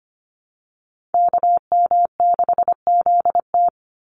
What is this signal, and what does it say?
Morse code, heard as a keyed tone, about 25 wpm, 700 Hz. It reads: KM6ZT
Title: Call Signs - 25wpm